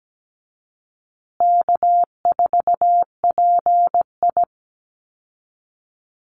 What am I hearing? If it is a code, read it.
Morse code, heard as a keyed tone, about 17 wpm, 700 Hz. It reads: K4PI
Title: Easy Call Signs Repeated 3x - 17wpm